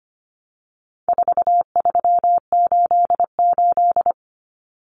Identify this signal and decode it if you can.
Morse code, heard as a keyed tone, about 25 wpm, 700 Hz. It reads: 4388
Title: Common Cut Numbers - 4-Digit 25wpm